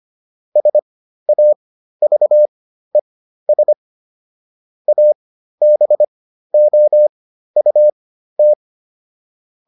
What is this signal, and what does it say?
Morse code, heard as a keyed tone, about 25 wpm, spaced as for 13 wpm, 600 Hz. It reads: SAVES ABOUT